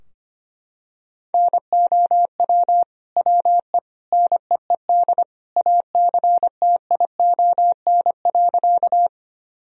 Morse code, heard as a keyed tone, about 25 wpm, 700 Hz. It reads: NOW WE NEED ACTION.
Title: Sentences from Top 1000 Words - Review Part 4 - 25wpm